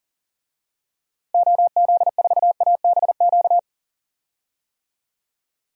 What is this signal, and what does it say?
Morse code, heard as a keyed tone, about 39 wpm, 700 Hz. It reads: OZ4ABQ
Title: DXCC-100 Call Signs - 6 Characters - Repeated 2x - 40wpm